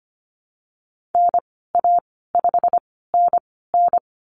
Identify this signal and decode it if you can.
Morse code, heard as a keyed tone, about 25 wpm, spaced as for 16 wpm, 700 Hz. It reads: NA5NN
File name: Lesson 15g - Number 5 - Call Signs 25wpm